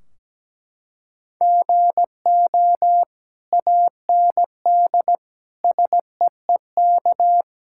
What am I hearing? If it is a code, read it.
Morse code, heard as a keyed tone, about 17 wpm, 700 Hz. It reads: GO AND SEEK
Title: Sentences from Top 1600 Words 17wpm